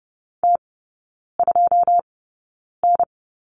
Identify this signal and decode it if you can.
Morse code, heard as a keyed tone, about 30 wpm, spaced as for 9 wpm, 700 Hz. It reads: T2N
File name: 081 Lesson 14e - Number 2 - Three Characters 30wpm